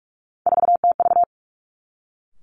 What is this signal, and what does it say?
Morse code, heard as a keyed tone, about 46 wpm, 700 Hz. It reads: 4TV
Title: Three Challenging Letter-Number Combinations 45wpm